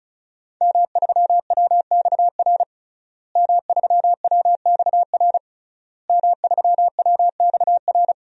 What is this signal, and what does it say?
Morse code, heard as a keyed tone, about 36 wpm, 700 Hz. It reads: M3WXR M3WXR M3WXR
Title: DXCC-100 Call Signs - 5 Characters - Repeated 3x - 35wpm